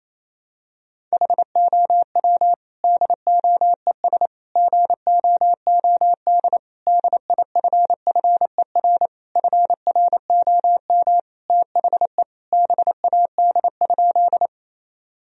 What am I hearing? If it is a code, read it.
Morse code, heard as a keyed tone, about 28 wpm, 700 Hz. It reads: HOW DOES GOOD DIFFER FROM THE BAD?
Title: Sentences from Top 1000 Words - Review Part 1 - 28wpm